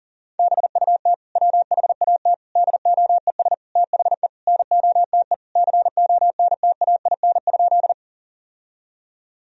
Morse code, heard as a keyed tone, about 40 wpm, 700 Hz. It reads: BUT WHAT DOES THE NOTE CONTAIN?
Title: Sentences from Top 1000 Words - Review Part 2 - 40wpm